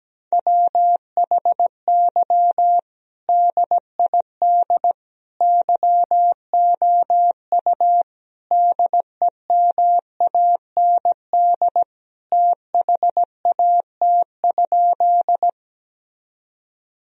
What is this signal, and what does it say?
Morse code, heard as a keyed tone, about 17 wpm, 700 Hz. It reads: WHY DID YOU DEMAND THAT?
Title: Sentences from Top 1000 Words - Review Part 4 - 17wpm